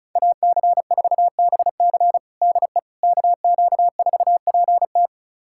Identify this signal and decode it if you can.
Morse code, heard as a keyed tone, about 34 wpm, 700 Hz. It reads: AC4BC DE KQ4PT
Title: USA Callsign DE Callsign - 5 Characters - 35wpm